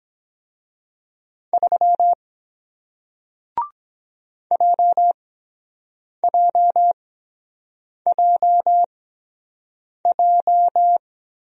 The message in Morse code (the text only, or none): 3 JJJJ